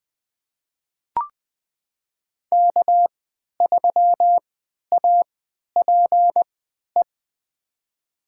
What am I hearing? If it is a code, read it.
Morse code, heard as a keyed tone, about 21 wpm, spaced as for 12 wpm, 700 Hz. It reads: K3APE
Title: USA Call Signs - 5 Characters - 3x Character Spacing - 20wpm